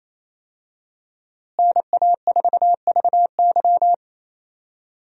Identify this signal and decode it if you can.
Morse code, heard as a keyed tone, about 28 wpm, 700 Hz. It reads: NA4VY
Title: USA Call Signs - 5 Characters - Repeated 2x - 28wpm